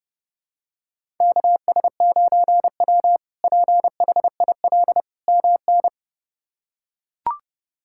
Morse code, heard as a keyed tone, about 30 wpm, 700 Hz. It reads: KS9W PHIL MN
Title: K1USN SST QSO Practice - 30wpm